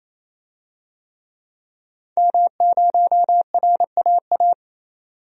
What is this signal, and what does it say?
Morse code, heard as a keyed tone, about 28 wpm, 700 Hz. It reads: M0RAA
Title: DXCC-100 Call Signs - 5 Characters - 28wpm